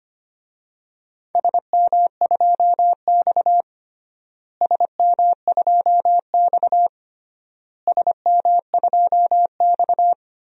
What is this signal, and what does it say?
Morse code, heard as a keyed tone, about 25 wpm, 700 Hz. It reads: SM2X SM2X SM2X
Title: DXCC-100 Call Signs - 4 Characters - Repeated 3x - 25wpm